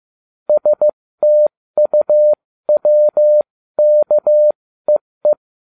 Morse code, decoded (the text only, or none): STUWKEE